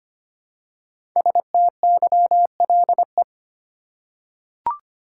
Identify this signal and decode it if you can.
Morse code, heard as a keyed tone, about 25 wpm, 700 Hz. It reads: STYLE
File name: 5 Letter Words - First 3 Letters Given - 25wpm